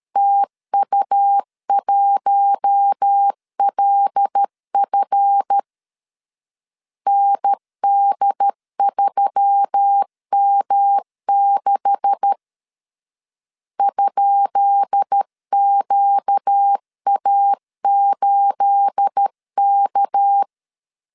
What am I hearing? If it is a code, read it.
Morse code, heard as a keyed tone, about 13 wpm, 800 Hz. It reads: TU1LF ND3M6 ?QA8K
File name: LEZIONI / 10 SIGNS / 10B INT 12wpm 2414